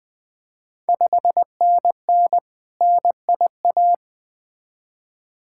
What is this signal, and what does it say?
Morse code, heard as a keyed tone, about 20 wpm, 690 Hz. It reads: 5NN NIA
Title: Ontario QSO Party Exchange 20wpm